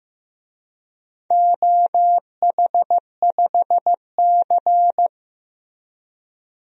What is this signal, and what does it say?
Morse code, heard as a keyed tone, about 15 wpm, 700 Hz. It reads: OH5C
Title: DXCC-100 Call Signs - 4 Characters - Repeated 3x - 15wpm